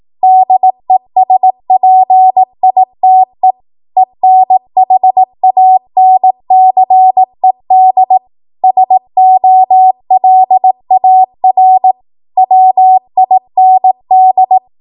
Morse code, heard as a keyed tone, about 18 wpm, 750 Hz. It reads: DESPITE ENHANCED SOLAR WIND
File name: Morse / 18WPM